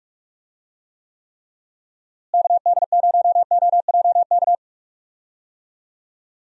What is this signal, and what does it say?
Morse code, heard as a keyed tone, about 46 wpm, 700 Hz. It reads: KD0OJK
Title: USA Call Signs - 6 Characters - Repeated 3x - 45wpm